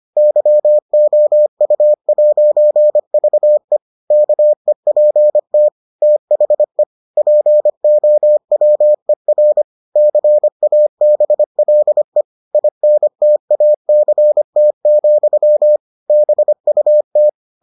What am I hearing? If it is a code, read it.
Morse code, heard as a keyed tone, about 25 wpm, 600 Hz. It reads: YOU'VE KEPT THE POWER CABLE INTACT, BUT